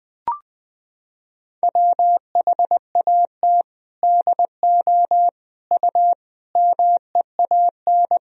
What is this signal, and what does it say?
Morse code, heard as a keyed tone, about 20 wpm, 700 Hz. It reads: WHAT DO U MEAN
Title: Sentences from Top 400 Words and Abbreviations 20wpm